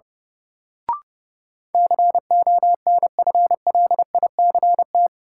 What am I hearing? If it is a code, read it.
Morse code, heard as a keyed tone, about 30 wpm, 700 Hz. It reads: CONFLICT
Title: Sentences from Top 1300 Words - New Word Drill Alt - 30wpm